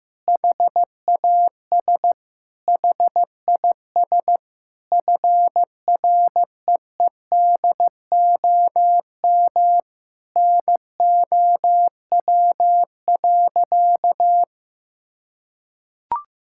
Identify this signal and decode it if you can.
Morse code, heard as a keyed tone, about 15 wpm, 700 Hz. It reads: HAS HIS FREEDOM NOW.
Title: Sentences from Top 1000 Words - Review Part 5 - 15wpm